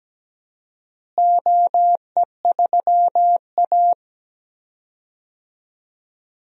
Morse code, heard as a keyed tone, about 17 wpm, 700 Hz. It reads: OE3A
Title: DXCC-100 Call Signs - 4 Characters - Repeated 2x - 17wpm